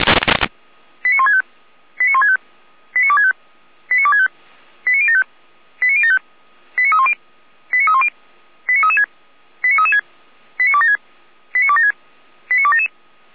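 Five-tone selective call (ZVEI), heard as quick runs of five tones, tones of about 70 ms. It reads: 89165 89165 89265 89265 89075 89075 89310 89310 89336 89336 89276 89276 89277
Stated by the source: ZVEI five-tone call received by SDR.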